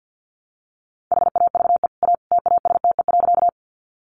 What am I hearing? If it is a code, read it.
Morse code, heard as a keyed tone, about 50 wpm, 700 Hz. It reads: HAVE A TASTE.
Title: Sentences from Top 1100 Words 50wpm